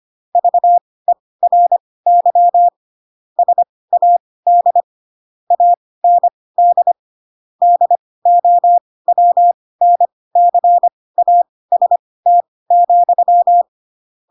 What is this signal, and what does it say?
Morse code, heard as a keyed tone, about 25 wpm, spaced as for 18 wpm, 700 Hz. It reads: VERY SAD AND DOWNCAST,
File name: Pinocchio ch3 12 25 ik1hge